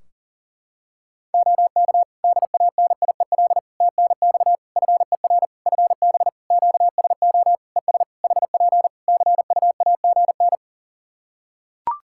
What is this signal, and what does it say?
Morse code, heard as a keyed tone, about 40 wpm, 700 Hz. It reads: OK DANIEL TNX FER FB QSO ES HP CUAGN
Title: Practice QSO - Exchange 4 - Ending - 40wpm